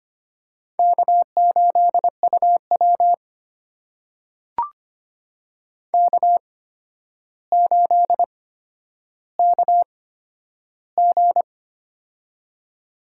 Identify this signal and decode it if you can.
Morse code, heard as a keyed tone, about 25 wpm, spaced as for 11 wpm, 700 Hz. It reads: K8UW K8KG
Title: USA Call Signs - 4 Characters - 8x Character Spacing - 25wpm